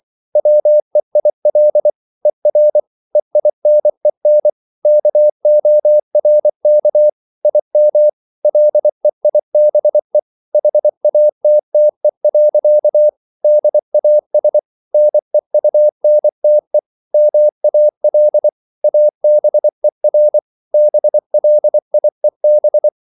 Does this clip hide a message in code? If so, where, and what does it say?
Yes: Morse code from 0.4 to 22.9 s, heard as a keyed tone, about 24 wpm, 600 Hz. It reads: WEIL ER EINEN KORK IM LEIBE HATTE. DAS NEUNTE MAL ABER BLIEB